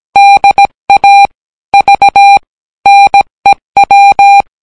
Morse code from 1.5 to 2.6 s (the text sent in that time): V